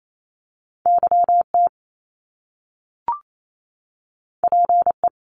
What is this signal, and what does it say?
Morse code, heard as a keyed tone, about 28 wpm, 700 Hz. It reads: YT PE
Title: Canadian Provinces and Territories 28wpm